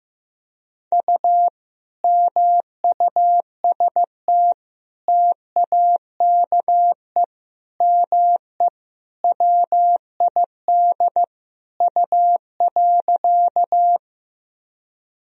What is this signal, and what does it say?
Morse code, heard as a keyed tone, about 15 wpm, 700 Hz. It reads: U MUST TAKE ME WID U.
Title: Sentences from Top 200 Words and Abbreviations 15wpm